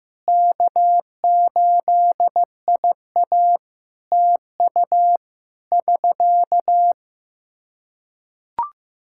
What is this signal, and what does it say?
Morse code, heard as a keyed tone, about 15 wpm, 700 Hz. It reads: K8IA TU <SK>
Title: K1USN SST QSO Practice 15wpm